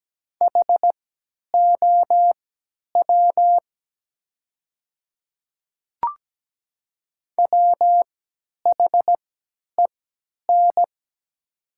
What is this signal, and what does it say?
Morse code, heard as a keyed tone, about 17 wpm, spaced as for 10 wpm, 700 Hz. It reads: HOW WHEN
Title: 3x Character Spacing - Top 100 Words - 17wpm